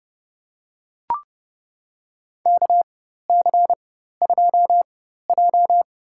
Morse code, heard as a keyed tone, about 30 wpm, spaced as for 14 wpm, 700 Hz. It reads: KC2J
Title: USA Call Signs - 4 Characters - 4x Character Spacing - 30wpm